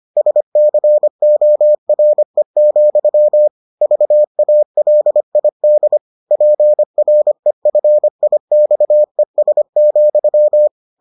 Morse code, heard as a keyed tone, about 25 wpm, 600 Hz. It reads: SCORE, VALID PREFIXES,